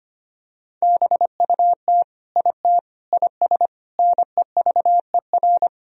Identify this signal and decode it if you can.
Morse code, heard as a keyed tone, about 25 wpm, 700 Hz. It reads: BUT IT IS NEVER
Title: Sentences from Top 1000 Words - Review Part 2 - 25wpm